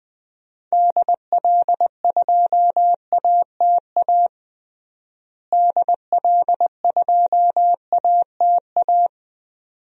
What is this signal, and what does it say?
Morse code, heard as a keyed tone, about 20 wpm, 700 Hz. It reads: DL2ATA DL2ATA
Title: DXCC-100 Call Signs - 6 Characters - Repeated 2x - 20wpm